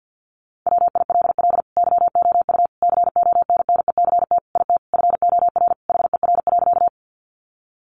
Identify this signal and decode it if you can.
Morse code, heard as a keyed tone, about 50 wpm, 700 Hz. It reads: WILL YOU CONNECT IT FOR HER.
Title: Sentences from Top 1500 Words 50wpm